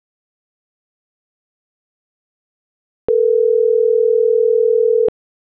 Ringback tone (North America), 440 + 480 Hz.